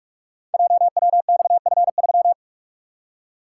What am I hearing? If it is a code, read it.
Morse code, heard as a keyed tone, about 44 wpm, 690 Hz. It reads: JWXF3
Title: Code Groups - 5 Letter:Numbers - 45wpm